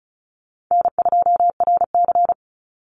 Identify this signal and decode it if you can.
Morse code, heard as a keyed tone, about 34 wpm, 700 Hz. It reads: N2RC
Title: Easy Call Signs Repeated 2x - 35wpm